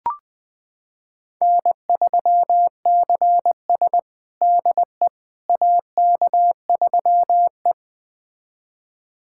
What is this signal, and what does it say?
Morse code, heard as a keyed tone, about 20 wpm, 700 Hz. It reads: N3CS DE AK3E
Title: USA Callsign DE Callsign - 4 Characters - 20wpm